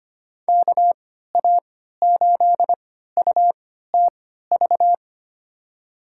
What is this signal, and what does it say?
Morse code, heard as a keyed tone, about 25 wpm, spaced as for 14 wpm, 700 Hz. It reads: KA8UTV
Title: USA Call Signs - 6 Characters - 3x Character Spacing - 25wpm